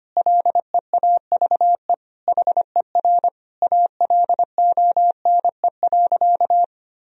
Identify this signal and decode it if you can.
Morse code, heard as a keyed tone, about 25 wpm, 700 Hz. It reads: LEAVE HER ALONE.
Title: Sentences from Top 1000 Words - Review Part 4 - 25wpm